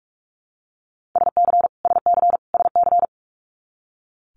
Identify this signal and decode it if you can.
Morse code, heard as a keyed tone, about 46 wpm, 700 Hz. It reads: SC SC SC
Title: Multi-letter Phonemes Repeated 3x 45wpm